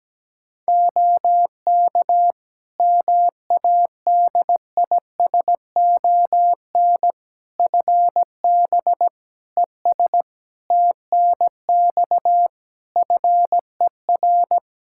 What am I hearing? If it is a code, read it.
Morse code, heard as a keyed tone, about 17 wpm, 700 Hz. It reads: OK MADISON FB ES TNX FER
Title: Practice QSO - Exchange 2 - Info - 17wpm